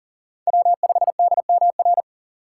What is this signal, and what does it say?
Morse code, heard as a keyed tone, about 40 wpm, 700 Hz. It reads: W5DMR
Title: USA Call Signs - 2x Character Spacing - 40wpm